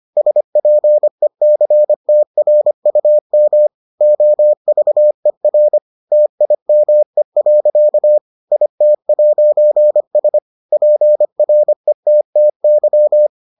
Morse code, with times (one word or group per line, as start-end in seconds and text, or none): 0.2-3.7 SPECTRUM
4.0-5.8 OVER
6.1-8.2 TIME.
8.5-10.4 IT'S
10.7-13.3 PRETTY